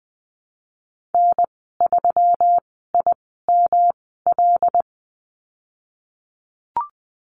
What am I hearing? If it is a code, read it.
Morse code, heard as a keyed tone, about 20 wpm, spaced as for 14 wpm, 700 Hz. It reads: N3IML
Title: 128 Lesson 21g - Number 6 - Call Signs 20wpm